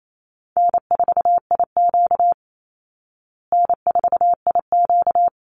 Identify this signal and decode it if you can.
Morse code, heard as a keyed tone, about 28 wpm, 700 Hz. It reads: N4IQ N4IQ
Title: Easy Call Signs Repeated 2x - 28wpm